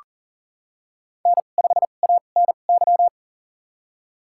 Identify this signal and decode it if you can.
Morse code, heard as a keyed tone, about 40 wpm, spaced as for 29 wpm, 700 Hz. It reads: N5ANY
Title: USA Call Signs - 5 Characters - 2x Character Spacing - 40wpm